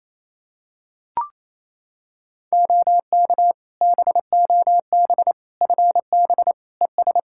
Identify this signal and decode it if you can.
Morse code, heard as a keyed tone, about 28 wpm, 700 Hz. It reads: OK BOB FB ES
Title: Practice QSO - Exchange 2 - Info - 28wpm